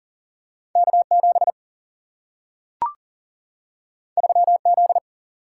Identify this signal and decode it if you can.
Morse code, heard as a keyed tone, about 38 wpm, 700 Hz. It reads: K7 3Z
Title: Single-Letter Single-Number 40wpm